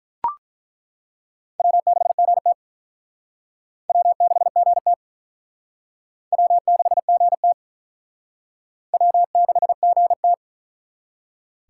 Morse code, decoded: W6GT W6GT W6GT W6GT

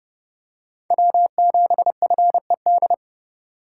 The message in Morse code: W7FED

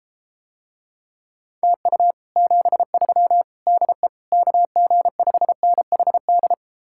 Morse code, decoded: TU 73 DE KG5NHD

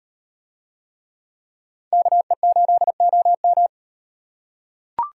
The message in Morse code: KE8OM

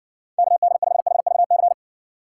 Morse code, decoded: KD5HVC

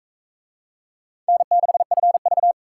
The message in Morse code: N6FV

